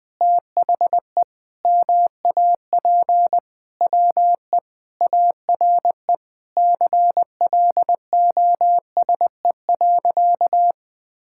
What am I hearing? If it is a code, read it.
Morse code, heard as a keyed tone, about 20 wpm, 700 Hz. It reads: THE MAP WE ARE CLOSE.
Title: Sentences with 2-Word Phrases - Spoken Beginning - 20wpm